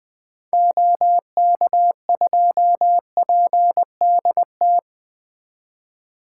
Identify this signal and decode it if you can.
Morse code, heard as a keyed tone, about 20 wpm, 700 Hz. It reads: OK2PDT
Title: 2023 SOTA Activator Call Signs - 20wpm